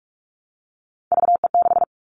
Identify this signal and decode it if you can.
Morse code, heard as a keyed tone, about 44 wpm, 700 Hz. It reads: VE6
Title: Callsign Prefixes for Top 50 DXCC - 45wpm